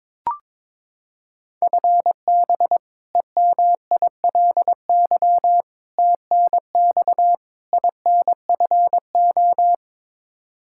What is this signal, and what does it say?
Morse code, heard as a keyed tone, about 22 wpm, 700 Hz. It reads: FB EMILY TNX INFO
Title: Practice QSO - Exchange 3 - More Info - 22wpm